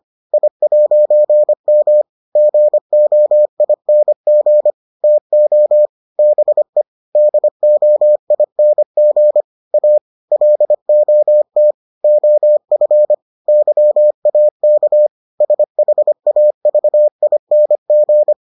Morse code, heard as a keyed tone, about 25 wpm, 600 Hz. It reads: I'M GOING TO BE DOING A LOT OF YAK SHAVING